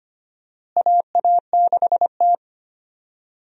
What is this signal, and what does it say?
Morse code, heard as a keyed tone, about 25 wpm, 700 Hz. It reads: AA6T
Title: USA Call Signs - 4 Characters - Repeated 2x - 25wpm